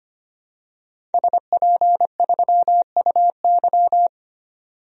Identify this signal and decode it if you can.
Morse code, heard as a keyed tone, about 25 wpm, 700 Hz. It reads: SP3UY
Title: DXCC-100 Call Signs - 5 Characters - 25wpm